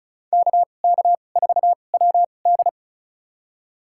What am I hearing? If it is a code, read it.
Morse code, heard as a keyed tone, about 36 wpm, spaced as for 25 wpm, 700 Hz. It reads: KK4WD
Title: USA Call Signs - 5 Characters - 2x Character Spacing - 35wpm